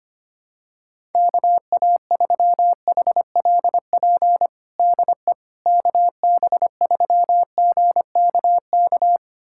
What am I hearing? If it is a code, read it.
Morse code, heard as a keyed tone, about 25 wpm, 700 Hz. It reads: KA3HLP DE KB3GKK
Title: USA Callsign DE Callsign 25wpm